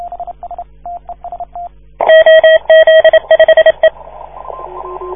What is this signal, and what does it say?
Morse code, heard as a keyed tone, about 27 wpm, 640 Hz. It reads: OZ5E
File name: 1esempio-telegrafia